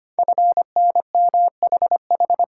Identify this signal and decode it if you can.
Morse code, heard as a keyed tone, about 25 wpm, 700 Hz. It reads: FNMHH
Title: Code Groups - 5 Letters - 25wpm